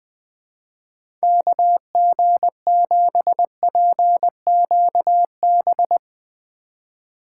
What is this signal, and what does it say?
Morse code, heard as a keyed tone, about 20 wpm, 700 Hz. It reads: KG7PQB